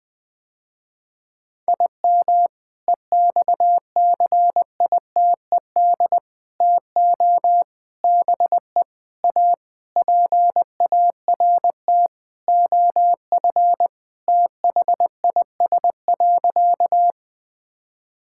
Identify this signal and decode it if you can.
Morse code, heard as a keyed tone, about 20 wpm, 700 Hz. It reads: IM EXCITED TO BE A PART OF THIS.
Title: Sentences with 5-Word Phrases - Spoken Beginning - 20wpm